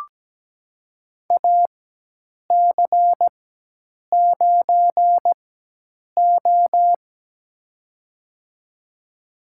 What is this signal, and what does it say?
Morse code, heard as a keyed tone, about 17 wpm, spaced as for 8 wpm, 700 Hz. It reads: AC9O
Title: USA Call Signs - 4 Characters - 4x Character Spacing - 17wpm